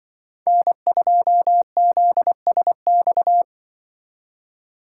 Morse code, heard as a keyed tone, about 24 wpm, 700 Hz. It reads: N2ZSX
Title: Speed Racing - USA 1x3 Call Signs 20wpm